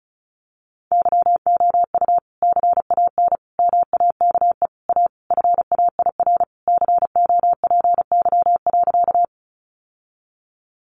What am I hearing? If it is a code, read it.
Morse code, heard as a keyed tone, about 34 wpm, 700 Hz. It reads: YOU CAN MAKE A FAIR COPY.
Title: Sentences from Top 1400 Words 35wpm